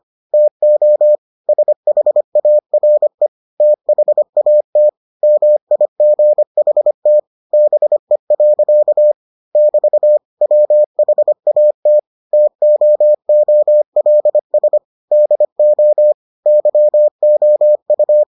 Morse code, heard as a keyed tone, about 25 wpm, 600 Hz. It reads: TO SHARE THAT MIGHT BE. <BT> WHAT TOOLS DO YOU